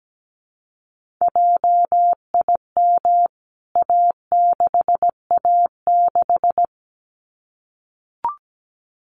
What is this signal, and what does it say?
Morse code, heard as a keyed tone, about 17 wpm, 700 Hz. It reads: JIM A6A6
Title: CWT Exchange 17wpm